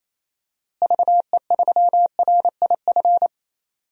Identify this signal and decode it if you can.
Morse code, heard as a keyed tone, about 28 wpm, 700 Hz. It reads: VE3RIF
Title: DXCC-100 Call Signs - 6 Characters - Repeated 3x - 28wpm